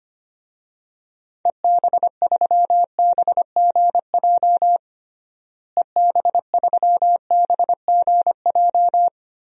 Morse code, heard as a keyed tone, about 25 wpm, 700 Hz. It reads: EB3BGJ EB3BGJ
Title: DXCC-100 Call Signs - 6 Characters - Repeated 3x - 25wpm